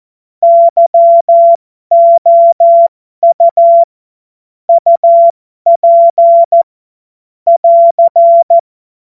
Morse code, heard as a keyed tone, about 14 wpm, spaced as for 12 wpm, 680 Hz. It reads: YOU UP <AR>
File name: Morsekurs / Witze